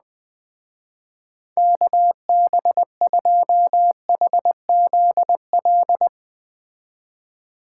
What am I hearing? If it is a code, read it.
Morse code, heard as a keyed tone, about 20 wpm, 700 Hz. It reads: KB2HZL